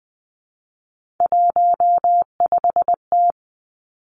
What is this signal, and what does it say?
Morse code, heard as a keyed tone, about 20 wpm, 700 Hz. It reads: 15T
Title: Three Challenging Letter-Number Combinations 20wpm